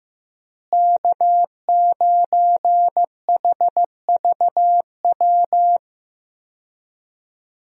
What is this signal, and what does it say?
Morse code, heard as a keyed tone, about 15 wpm, 700 Hz. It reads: K9HVW
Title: USA Call Signs - 5 Characters - Repeated 2x - 15wpm